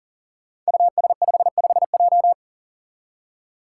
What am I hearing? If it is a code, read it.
Morse code, heard as a keyed tone, about 42 wpm, 700 Hz. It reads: US55J
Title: Five Challenging Letter-Number Combinations 40wpm